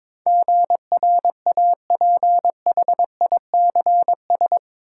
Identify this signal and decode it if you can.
Morse code, heard as a keyed tone, about 22 wpm, 700 Hz. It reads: GRAPHICS
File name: practice / 22wpm / Eight Letter Words - 22wpm